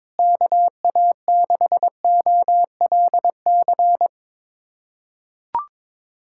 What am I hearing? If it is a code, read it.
Morse code, heard as a keyed tone, about 22 wpm, 700 Hz. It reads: KA6OLC
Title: USA Callsign DE Callsign - 6 Characters - 22wpm